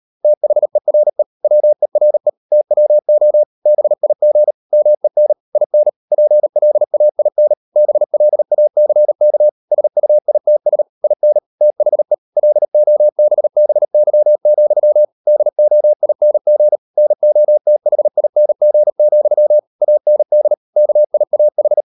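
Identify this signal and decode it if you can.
Morse code, heard as a keyed tone, about 38 wpm, 600 Hz. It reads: THERE WERE TWO BIG MEN IN PLAIN BLACK SUITS IN THE LOBBY, DOING NOTHING, AND KIAH
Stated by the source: Morse code practice at 38 WPM, sending Ken Follett's novel Never in English, from CW med Gnister.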